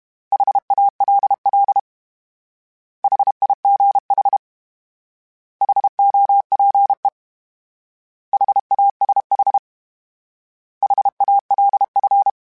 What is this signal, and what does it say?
Morse code, heard as a keyed tone, about 32 wpm, 800 Hz. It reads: HALL HIGH HOPE HASH HALF
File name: H Prefixes / H mp3s